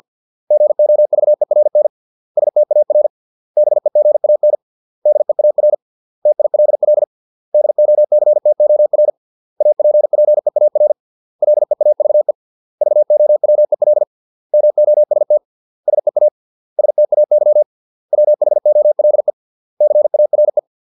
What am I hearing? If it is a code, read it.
Morse code, heard as a keyed tone, about 50 wpm, 600 Hz. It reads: GOVERN STAR BEGAN DEAR TILL DOCTOR APPEAR LEAVE VOWEL MOST SEA STAY WHOLE CARE